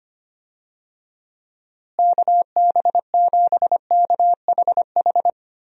Morse code, heard as a keyed tone, about 25 wpm, 700 Hz. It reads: KB7KHH
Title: USA Call Signs - 6 Characters - Repeated 2x - 25wpm